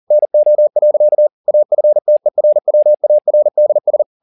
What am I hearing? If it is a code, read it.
Morse code, heard as a keyed tone, about 40 wpm, 600 Hz. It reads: NO. AFTERWARDS
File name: the old man and the sea 2 40wpm